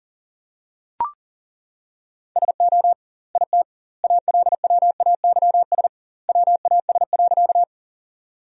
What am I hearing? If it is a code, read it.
Morse code, heard as a keyed tone, about 40 wpm, 700 Hz. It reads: SO IT ALWAYS WAS.
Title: 2x Word-Spacing - Sentences from Top 300 Words - 40wpm